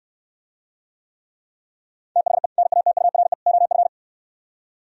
Morse code, heard as a keyed tone, about 68 wpm, 700 Hz. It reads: THE NATURE OF